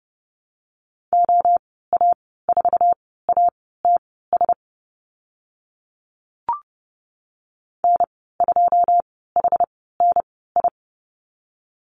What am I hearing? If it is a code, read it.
Morse code, heard as a keyed tone, about 30 wpm, spaced as for 17 wpm, 690 Hz. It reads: OA4ATS N2HNI
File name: Lesson 15g - Number 5 - Call Signs 30wpm